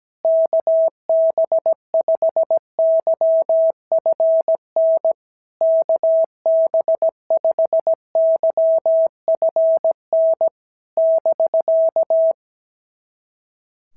Morse code, heard as a keyed tone, about 17 wpm, 650 Hz. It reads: KB5YFN KB5YFN <BK>
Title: SOTA QSO Practice 17wpm